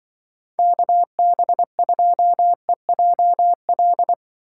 Morse code, heard as a keyed tone, about 23 wpm, 700 Hz. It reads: KB2EJL
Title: Speed Racing - USA 2x3 Call Signs 20wpm